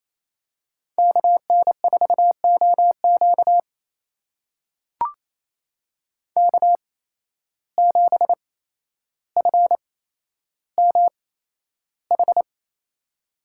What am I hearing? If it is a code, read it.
Morse code, heard as a keyed tone, about 28 wpm, spaced as for 12 wpm, 700 Hz. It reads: KN4OQ K7FMH
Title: USA Call Signs - 5 Characters - 8x Character Spacing - 28wpm